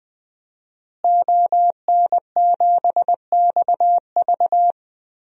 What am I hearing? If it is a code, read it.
Morse code, heard as a keyed tone, about 20 wpm, 700 Hz. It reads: ON7XV
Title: DXCC-100 Call Signs - 5 Characters - Repeated 2x - 20wpm